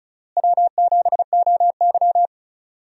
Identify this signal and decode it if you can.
Morse code, heard as a keyed tone, about 36 wpm, 700 Hz. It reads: W7OY
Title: USA Call Signs - 4 Characters - Repeated 3x - 35wpm